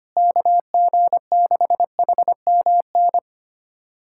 Morse code, heard as a keyed tone, about 25 wpm, 700 Hz. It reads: KG6HMN
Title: USA Callsign DE Callsign 25wpm